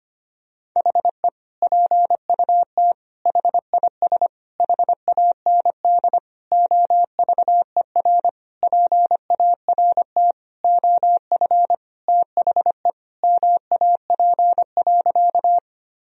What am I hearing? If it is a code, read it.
Morse code, heard as a keyed tone, about 25 wpm, 700 Hz. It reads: HE PUT HIS HAND OVER PART OF THE MAP.
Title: Sentences from Top 1000 Words - Review Part 2 - 25wpm